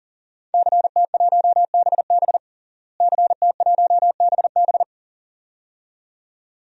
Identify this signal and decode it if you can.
Morse code, heard as a keyed tone, about 40 wpm, 700 Hz. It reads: CT1BB CT1BB
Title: DXCC-100 Call Signs - 5 Characters - Repeated 2x - 40wpm